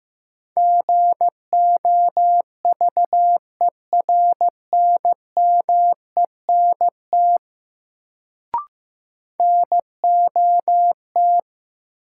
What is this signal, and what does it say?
Morse code, heard as a keyed tone, about 15 wpm, 700 Hz. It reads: GOVERNMENT NOT
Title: Sentences from Top 600 Words - New Word Drill - 15wpm